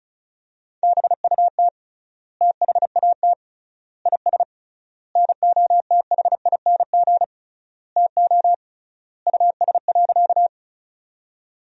Morse code, heard as a keyed tone, about 36 wpm, 700 Hz. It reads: BUT THAT IS NOTHING TO US.
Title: 3x Word-Spacing - Sentences from Top 500 Words - 35wpm